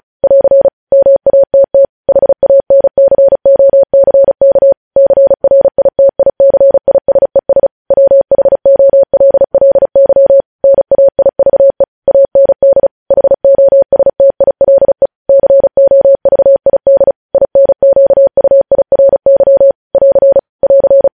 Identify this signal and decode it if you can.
Morse code, heard as a keyed tone, about 36 wpm, 550 Hz. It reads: <AR> MATT HANCOCK CRITICISES WHOLLY NAIVE AND HOSTILE COVID INQUIRY <AR> <AR>
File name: podcast / News headlines-1x-35x35